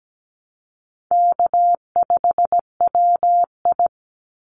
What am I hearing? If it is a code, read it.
Morse code, heard as a keyed tone, about 17 wpm, 700 Hz. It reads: K5WI
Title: NA SOTA Chaser Callsigns 17wpm